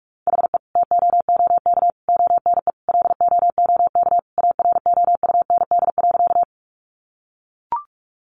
Morse code, heard as a keyed tone, about 44 wpm, 700 Hz. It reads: HE TOOK ONE LOOK AROUND.
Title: Sentences from Top 600 Words 45wpm